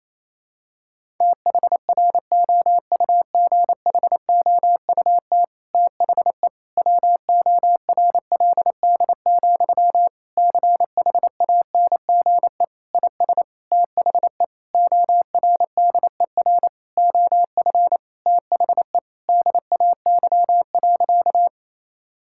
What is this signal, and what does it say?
Morse code, heard as a keyed tone, about 27 wpm, 700 Hz. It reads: THROUGHOUT THE WORLD, CHANGE IS THE ORDER OF THE DAY.
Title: Sentences from Top 1000 Words - Review Part 5 - 28wpm